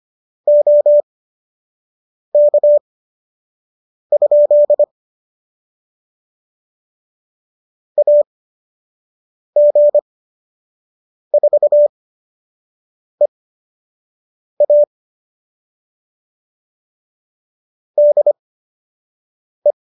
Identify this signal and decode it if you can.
Morse code, heard as a keyed tone, about 25 wpm, spaced as for 6 wpm, 590 Hz. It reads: OK? AG4EA DE